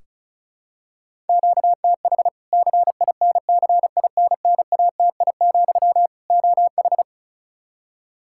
Morse code, heard as a keyed tone, about 36 wpm, 700 Hz. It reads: QTH CINCINNATI, OH
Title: Practice QSO - Exchange 1 - Beginning - 35wpm